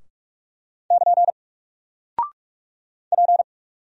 Morse code, heard as a keyed tone, about 46 wpm, 700 Hz. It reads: <KN> P
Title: Single Character (Letter:Number:Punctuation:Prosign) 45wpm